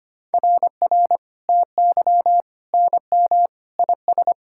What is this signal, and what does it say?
Morse code, heard as a keyed tone, about 25 wpm, 700 Hz. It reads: RR TY NM IS
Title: SOTA QSO Practice 25wpm